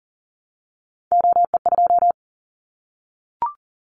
Morse code, heard as a keyed tone, about 40 wpm, 700 Hz. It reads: OE2
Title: Callsign Prefixes for Top 100 DXCC - 40wpm